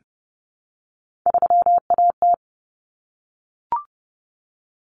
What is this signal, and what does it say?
Morse code, heard as a keyed tone, about 30 wpm, 700 Hz. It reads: 3AT
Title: Common Cut Numbers - 3-Digit 30wpm